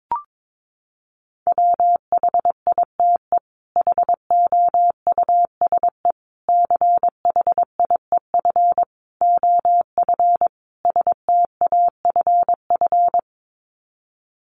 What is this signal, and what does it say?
Morse code, heard as a keyed tone, about 22 wpm, 700 Hz. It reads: WHITE HOUSE CHIEF OF STAFF
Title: Sets of 5 Words Encore 22wpm